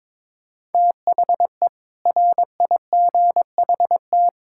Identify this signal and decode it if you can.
Morse code, heard as a keyed tone, about 22 wpm, 700 Hz. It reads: THE RIGHT
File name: practice / 22wpm / Build A Phrase - 5-Word Phrases - 22wpm